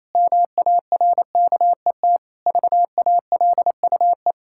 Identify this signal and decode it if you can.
Morse code, heard as a keyed tone, about 28 wpm, 700 Hz. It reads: MARKET VALUE
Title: Sentences from Top 1000 Words - Review Part 6 - 28wpm